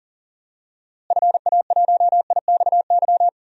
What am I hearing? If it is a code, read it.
Morse code, heard as a keyed tone, about 40 wpm, 700 Hz. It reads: FA1IXY
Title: DXCC-100 Call Signs - 6 Characters - Repeated 2x - 40wpm